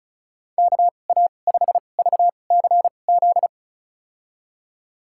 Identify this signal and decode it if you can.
Morse code, heard as a keyed tone, about 36 wpm, spaced as for 25 wpm, 700 Hz. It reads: KA5VCZ
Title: USA Call Signs - 6 Characters - 2x Character Spacing - 35wpm